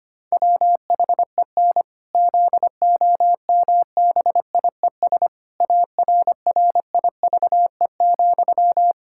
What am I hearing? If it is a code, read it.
Morse code, heard as a keyed tone, about 25 wpm, 700 Hz. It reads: WHEN ZOMBIES ARRIVE,
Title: Pangram Challenge v2 - 25wpm